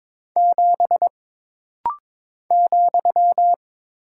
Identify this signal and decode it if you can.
Morse code, heard as a keyed tone, about 22 wpm, 700 Hz. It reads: 7 ,
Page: This rapid-fire timing is geared to push your ICR (Instant Character Recognition) skills.